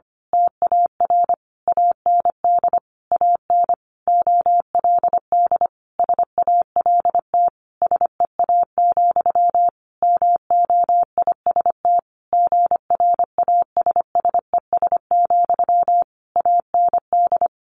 Morse code, heard as a keyed tone, about 25 wpm, 700 Hz. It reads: TAR AND AN OLD SALT SEA, MOIST GRASSES, AND
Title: A Sound Of Thunder 25wpm